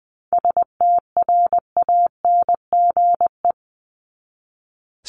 Morse code, heard as a keyed tone, about 20 wpm, 700 Hz. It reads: STRANGE
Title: Top 1501-1750 Words 20wpm